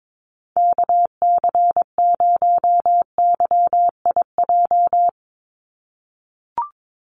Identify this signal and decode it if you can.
Morse code, heard as a keyed tone, about 22 wpm, 700 Hz. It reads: KC0YIJ
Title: USA Call Signs 22wpm